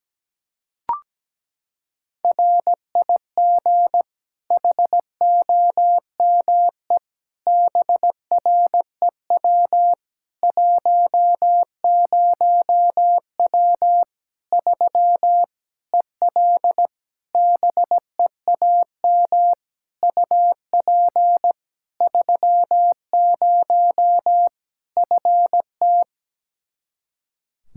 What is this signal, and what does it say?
Morse code, heard as a keyed tone, about 17 wpm, 700 Hz. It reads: RIG HOME BREW 10W 3 EL BEAM UP 30 FT
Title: Practice QSO - Exchange 2 - Info - 17wpm